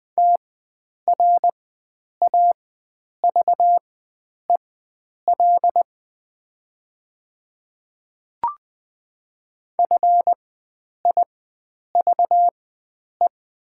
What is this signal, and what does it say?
Morse code, heard as a keyed tone, about 20 wpm, spaced as for 9 wpm, 700 Hz. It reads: TRAVEL FIVE
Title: 4x Character Spacing - Top 500 Words - 20wpm